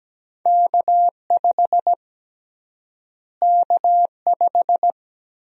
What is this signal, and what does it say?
Morse code, heard as a keyed tone, about 17 wpm, 700 Hz. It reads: K5 K5
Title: USA Call Sign Prefixes - Repeated 2x - 17wpm